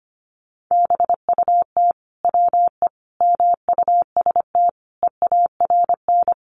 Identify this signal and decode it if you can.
Morse code, heard as a keyed tone, about 25 wpm, 700 Hz. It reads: BUT WE MUST EARN
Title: Sentences from Top 1500 Words 25wpm